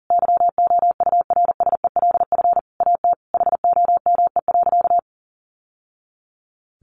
Morse code, heard as a keyed tone, about 40 wpm, 700 Hz. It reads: YOURSELF AT HOME.
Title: Sentences from Top 600 Words 40wpm